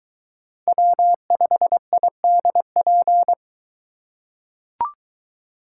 Morse code, heard as a keyed tone, about 21 wpm, 700 Hz. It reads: W5IDP